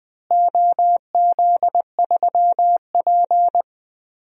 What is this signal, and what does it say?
Morse code, heard as a keyed tone, about 20 wpm, 700 Hz. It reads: OZ3P
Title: DXCC-100 Call Signs - 4 Characters - Repeated 3x - 20wpm